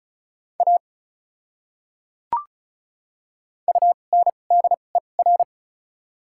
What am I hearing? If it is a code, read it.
Morse code, heard as a keyed tone, about 36 wpm, spaced as for 25 wpm, 700 Hz. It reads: A UNDER
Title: 2x Character Spacing - Top 200 Words - 35wpm